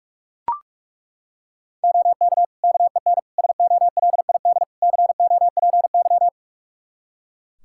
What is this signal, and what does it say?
Morse code, heard as a keyed tone, about 42 wpm, 700 Hz. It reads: OK KEN SOLID COPY
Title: Practice QSO - Exchange 3 - More Info - 45wpm